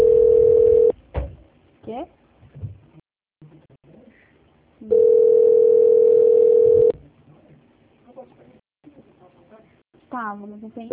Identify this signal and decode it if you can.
Ringback tone (North America), 440 + 480 Hz, with speech.
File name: out- -2033- - - .1639